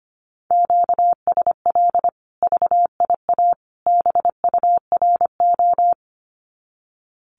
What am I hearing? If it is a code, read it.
Morse code, heard as a keyed tone, about 25 wpm, 700 Hz. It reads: QSL VIA BURO
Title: Common QSO Phrases 25wpm